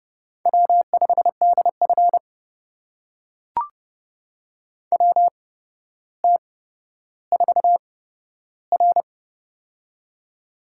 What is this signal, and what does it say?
Morse code, heard as a keyed tone, about 30 wpm, spaced as for 13 wpm, 700 Hz. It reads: W5DF WT4R
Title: USA Call Signs - 4 Characters - 8x Character Spacing - 30wpm